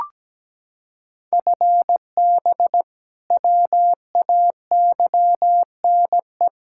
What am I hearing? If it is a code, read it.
Morse code, heard as a keyed tone, about 17 wpm, 700 Hz. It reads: FB WAYNE
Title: K1USN SST QSO Practice 17wpm